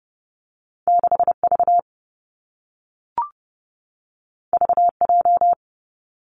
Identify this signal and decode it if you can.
Morse code, heard as a keyed tone, about 30 wpm, 700 Hz. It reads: 6 V V J
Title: Two Challenging Letter-Number Combinations 30wpm